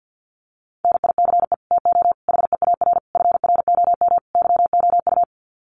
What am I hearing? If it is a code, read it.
Morse code, heard as a keyed tone, about 50 wpm, 700 Hz. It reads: NICE TO HEAR FROM YOU
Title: Common QSO Phrases 50wpm